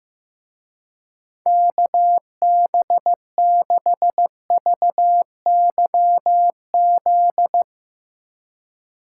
Morse code, heard as a keyed tone, about 15 wpm, 700 Hz. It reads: KB6VYZ
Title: Speed Racing - USA 2x3 Call Signs 15wpm